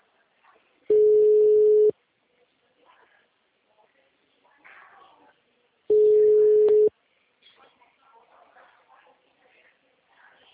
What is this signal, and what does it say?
Ringback tone (Europe, 425 Hz), with speech.
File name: out- -2005- - - .24040